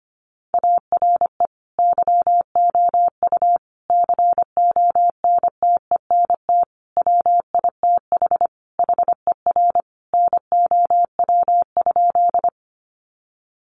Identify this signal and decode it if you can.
Morse code, heard as a keyed tone, about 25 wpm, 700 Hz. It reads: ARE YOU CONTENT WITH HER NOW?
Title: Sentences from Top 900 Words 25wpm